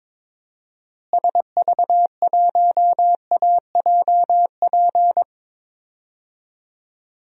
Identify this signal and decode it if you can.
Morse code, heard as a keyed tone, about 22 wpm, 700 Hz. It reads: SV1AJP
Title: DXCC-100 Call Signs - 6 Characters - Repeated 3x - 22wpm